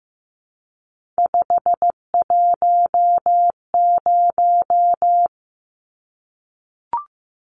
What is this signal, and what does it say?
Morse code, heard as a keyed tone, about 15 wpm, 700 Hz. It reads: 510
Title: Numbers - 3-Digit 15wpm